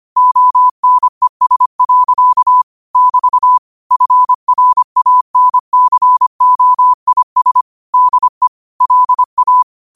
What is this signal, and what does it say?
Morse code, heard as a keyed tone, about 25 wpm, 1000 Hz. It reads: ONES. <BT> FRANCOIS DE LA
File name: qotdfiles / 25wpm